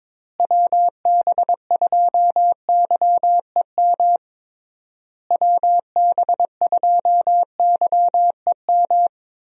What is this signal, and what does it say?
Morse code, heard as a keyed tone, about 22 wpm, 700 Hz. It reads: WB2YEM WB2YEM
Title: USA Call Signs - 6 Characters - Repeated 3x - 22wpm